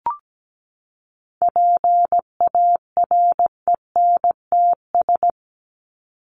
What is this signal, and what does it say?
Morse code, heard as a keyed tone, about 17 wpm, 700 Hz. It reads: PARENTS
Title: ES:S Suffix - Top 100 Words with Base Words 17wpm